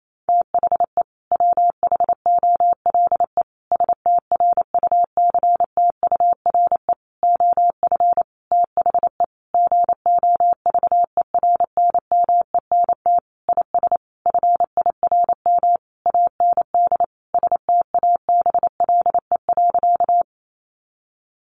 Morse code, heard as a keyed tone, about 28 wpm, 700 Hz. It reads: THE WHOLE STRUCTURE OF THE GOVERNMENT IS FIRM AND STABLE.
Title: Sentences from Top 1500 Words 28wpm